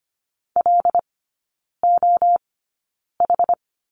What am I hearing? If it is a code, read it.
Morse code, heard as a keyed tone, about 25 wpm, spaced as for 9 wpm, 690 Hz. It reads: LOH
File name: Lesson 15e - Number 5 - Three Characters 25wpm